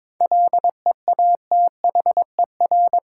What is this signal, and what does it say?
Morse code, heard as a keyed tone, about 22 wpm, 700 Hz. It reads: LEATHER
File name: Difficult Words with Hints - 22wpm